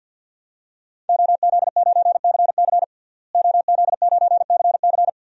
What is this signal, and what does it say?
Morse code, heard as a keyed tone, about 48 wpm, 700 Hz. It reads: OZ9CC OZ9CC
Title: DXCC-100 Call Signs - 5 Characters - Repeated 2x - 50wpm